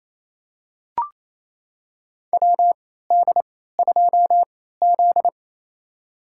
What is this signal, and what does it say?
Morse code, heard as a keyed tone, about 28 wpm, spaced as for 16 wpm, 700 Hz. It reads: WD2Z
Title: USA Call Signs - 4 Characters - 3x Character Spacing - 28wpm